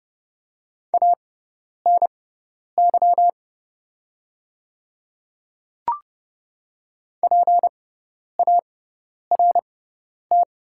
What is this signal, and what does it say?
Morse code, heard as a keyed tone, about 30 wpm, spaced as for 10 wpm, 700 Hz. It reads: ANY PART
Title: 6x Character Spacing - Top 200 Words - 30wpm